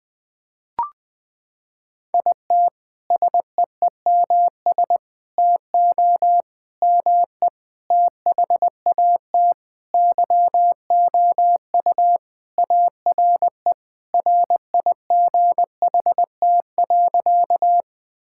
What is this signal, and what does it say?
Morse code, heard as a keyed tone, about 20 wpm, 700 Hz. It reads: IT SEEMS TO ME THAT YOU ARE RIGHT.
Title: Sentences with 5-Word Phrases - 20wpm